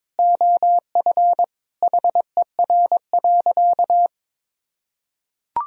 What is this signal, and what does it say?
Morse code, heard as a keyed tone, about 22 wpm, 700 Hz. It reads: OF HER.
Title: Sentences from Top 1000 Words - Review Part 4 - 22wpm